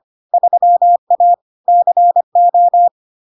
Morse code, heard as a keyed tone, about 25 wpm, 700 Hz. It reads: 3A CO
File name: LARK-FD-2022-Log-in-CW-25wpm-700-Hz